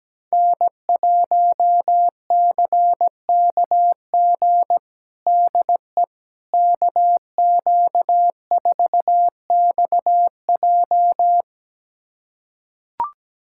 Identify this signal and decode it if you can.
Morse code, heard as a keyed tone, about 17 wpm, 700 Hz. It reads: N1CKG DE KQ4XJ
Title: USA Callsign DE Callsign - 5 Characters - 17wpm